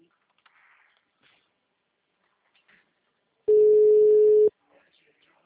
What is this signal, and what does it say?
Ringback tone (Europe, 425 Hz).